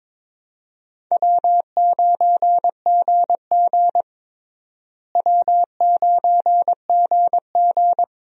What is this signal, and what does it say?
Morse code, heard as a keyed tone, about 22 wpm, 700 Hz. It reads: W9GG W9GG
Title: USA Call Signs - 4 Characters - Repeated 3x - 22wpm